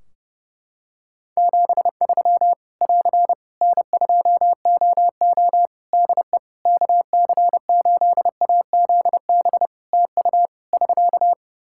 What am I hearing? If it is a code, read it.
Morse code, heard as a keyed tone, about 30 wpm, 700 Hz. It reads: 73 <AR> N2OO DE KC8AZB TU <SK>
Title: Practice QSO - Exchange 4 - Ending - 30wpm